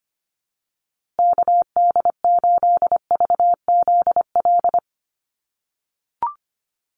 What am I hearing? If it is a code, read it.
Morse code, heard as a keyed tone, about 25 wpm, 700 Hz. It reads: KD8VZL
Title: USA Call Signs Repeated 3x 25wpm